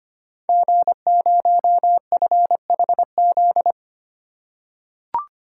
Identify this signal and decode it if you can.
Morse code, heard as a keyed tone, about 25 wpm, 700 Hz. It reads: G0FHZ
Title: Code Groups - 5 Letter:Numbers - 25wpm